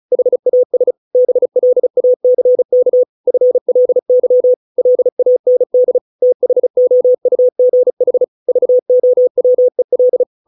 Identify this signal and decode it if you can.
Morse code, heard as a keyed tone, about 34 wpm, 500 Hz. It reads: HAS BLACK FLY LAND THOUGH VOWEL